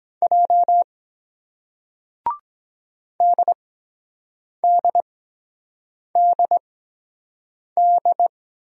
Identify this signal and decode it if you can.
Morse code, heard as a keyed tone, about 24 wpm, spaced as for 7 wpm, 700 Hz. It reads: J DDDD